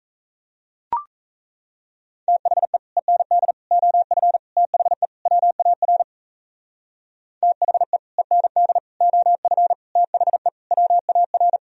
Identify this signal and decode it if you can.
Morse code, heard as a keyed tone, about 34 wpm, 700 Hz. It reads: THE END OF THE WAR THE END OF THE WAR
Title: Speed-Racing - Sets of 5 Words 28wpm